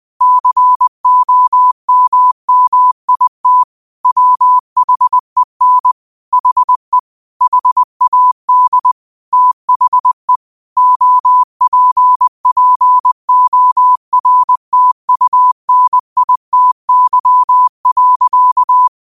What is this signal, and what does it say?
Morse code, heard as a keyed tone, about 20 wpm, 1000 Hz. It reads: COMMIT WHEN HE HAD THE OPPORTUNITY.